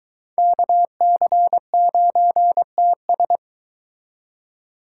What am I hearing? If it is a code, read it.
Morse code, heard as a keyed tone, about 23 wpm, 700 Hz. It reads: KC9TS